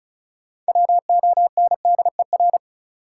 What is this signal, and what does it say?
Morse code, heard as a keyed tone, about 36 wpm, 700 Hz. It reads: WONDER
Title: Sentences from Top 500 Words - New Word Drill - 35wpm